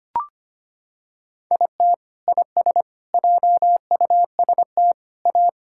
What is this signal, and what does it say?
Morse code, heard as a keyed tone, about 25 wpm, 700 Hz. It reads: IT IS JUST A
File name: practice / 25wpm / Sentences with 2-Word Phrases - 25wpm